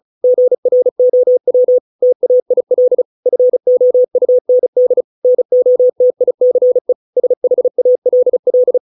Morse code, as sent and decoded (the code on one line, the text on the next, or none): --. .-. --- .-- / - .- .. .-.. / ..-. --- ..- -. -.. / -. --- - .. -.-. / ... .... .- .-.. .-..
GROW TAIL FOUND NOTICE SHALL